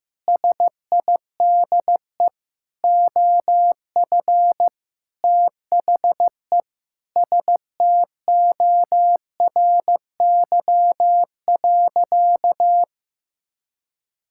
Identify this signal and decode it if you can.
Morse code, heard as a keyed tone, about 15 wpm, 700 Hz. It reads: SIDE OF THE STORY.
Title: Sentences from Top 1000 Words - Review Part 6 - 15wpm